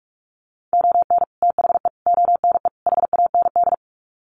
Morse code, heard as a keyed tone, about 44 wpm, 700 Hz. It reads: ON THE ONE HAND
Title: Sets of 4 Words 45wpm